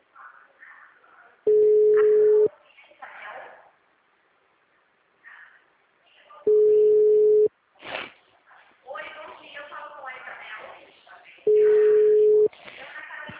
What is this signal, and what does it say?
Ringback tone (Europe, 425 Hz), with speech.
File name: out- -2045- - - .8195